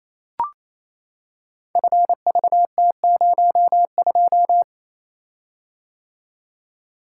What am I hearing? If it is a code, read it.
Morse code, heard as a keyed tone, about 28 wpm, 700 Hz. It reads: FVT02
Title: Code Groups - 5 Letter:Numbers - 28wpm